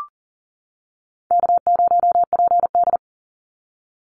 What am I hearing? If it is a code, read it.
Morse code, heard as a keyed tone, about 38 wpm, 700 Hz. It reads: K0PD
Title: Easy Call Signs - 40wpm